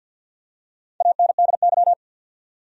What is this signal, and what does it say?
Morse code, heard as a keyed tone, about 51 wpm, 700 Hz. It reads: ANDY
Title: USA Common Names - 50wpm